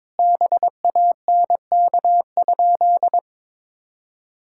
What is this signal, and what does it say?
Morse code, heard as a keyed tone, about 22 wpm, 700 Hz. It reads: BANK?
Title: Sentences from Top 1000 Words - Review Part 6 - 22wpm